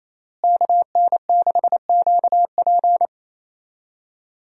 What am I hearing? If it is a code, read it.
Morse code, heard as a keyed tone, about 27 wpm, 700 Hz. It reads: KN6QP
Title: USA Call Signs - 5 Characters - Repeated 3x - 28wpm